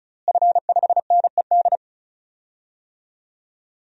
Morse code, heard as a keyed tone, about 34 wpm, 700 Hz. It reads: F5NED
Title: DXCC-100 Call Signs - 5 Characters - Repeated 3x - 35wpm